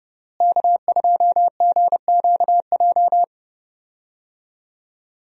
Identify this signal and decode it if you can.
Morse code, heard as a keyed tone, about 32 wpm, 700 Hz. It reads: K2GQJ
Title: USA Call Signs - 5 Characters - Repeated 2x - 30wpm